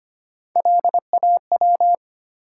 Morse code, heard as a keyed tone, about 25 wpm, 700 Hz. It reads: LAW
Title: Sentences from Top 600 Words - New Word Drill - 25wpm